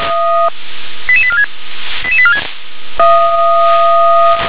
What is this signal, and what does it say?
Five-tone selective call (ZVEI), heard as quick runs of five tones, tones of about 70 ms. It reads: 88846 88846
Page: ZVEI